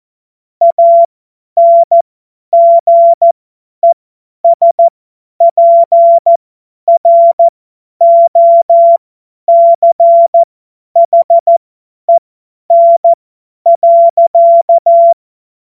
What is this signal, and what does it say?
Morse code, heard as a keyed tone, about 14 wpm, spaced as for 10 wpm, 680 Hz. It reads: ANGESPROCHEN.
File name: Morsekurs / Witze / Witz 08. August 2023 50 F70